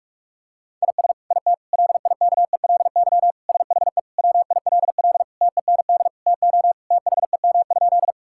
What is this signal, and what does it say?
Morse code, heard as a keyed tone, about 46 wpm, 700 Hz. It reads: IS IT LIKELY SHE WILL TEND TO THEM?
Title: Sentences from Top 1000 Words - Review Part 5 - 45wpm